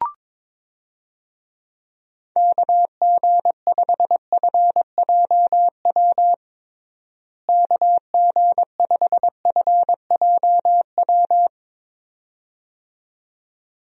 Morse code, heard as a keyed tone, about 22 wpm, 700 Hz. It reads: KG5FJW KG5FJW
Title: USA Call Signs - 6 Characters - Repeated 2x - 22wpm